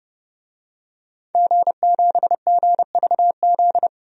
Morse code, heard as a keyed tone, about 30 wpm, 700 Hz. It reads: G7GVZ
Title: DXCC-100 Call Signs - 5 Characters - 30wpm